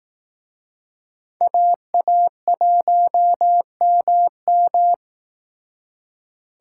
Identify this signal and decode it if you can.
Morse code, heard as a keyed tone, about 18 wpm, 700 Hz. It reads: AA1MM
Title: Speed Racing - USA 2x2 Call Signs 15wpm